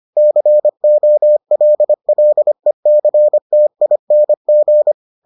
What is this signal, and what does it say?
Morse code, heard as a keyed tone, about 25 wpm, 600 Hz. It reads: COLLECTING